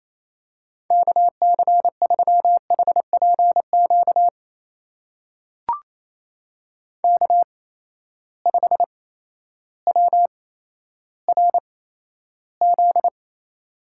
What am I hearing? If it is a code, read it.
Morse code, heard as a keyed tone, about 27 wpm, spaced as for 13 wpm, 700 Hz. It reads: KC3HPQ K5WRZ
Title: USA Call Signs - 8x Character Spacing - 28wpm